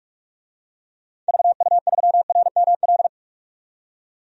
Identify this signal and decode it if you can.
Morse code, heard as a keyed tone, about 46 wpm, 700 Hz. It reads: VU3RML
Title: DXCC-100 Call Signs - 6 Characters - 45wpm